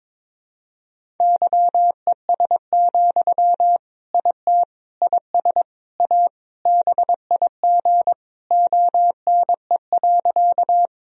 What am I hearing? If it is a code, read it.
Morse code, heard as a keyed tone, about 21 wpm, 700 Hz. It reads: YES, IT IS A BIG ONE.
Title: Sentences from Top 1000 Words - Review Part 1 - 22wpm